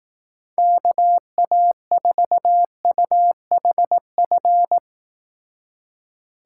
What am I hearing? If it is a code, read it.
Morse code, heard as a keyed tone, about 19 wpm, 700 Hz. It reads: KA4UHF
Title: Speed Racing - USA 2x3 Call Signs 15wpm